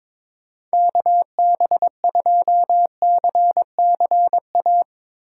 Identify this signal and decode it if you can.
Morse code, heard as a keyed tone, about 22 wpm, 700 Hz. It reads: KB2CCA
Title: USA Call Signs - 6 Characters - 2x Character Spacing - 22wpm